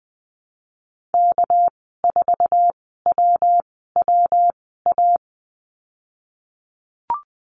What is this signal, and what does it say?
Morse code, heard as a keyed tone, about 20 wpm, spaced as for 15 wpm, 700 Hz. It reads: K4WWA
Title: 231 Lesson 37g - Letter x - Call Signs 20wpm